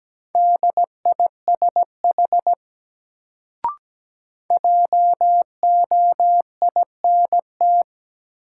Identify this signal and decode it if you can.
Morse code, heard as a keyed tone, about 17 wpm, 700 Hz. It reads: DISH JOINT
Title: Sentences from Top 1300 Words - New Word Drill - 17wpm